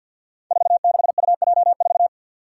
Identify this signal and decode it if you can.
Morse code, heard as a keyed tone, about 53 wpm, 700 Hz. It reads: 46UJ4
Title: Five Challenging Letter-Number Combinations 50wpm